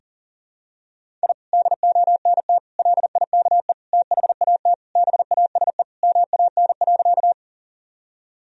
Morse code, heard as a keyed tone, about 37 wpm, 700 Hz. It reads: I DONT LIKE THAT BASE MAN.
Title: Sentences from Top 1000 Words - Review Part 1 - 40wpm